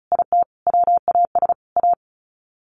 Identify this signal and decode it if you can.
Morse code, heard as a keyed tone, about 34 wpm, 700 Hz. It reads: IT WAS A
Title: Sets of 5 Words 35wpm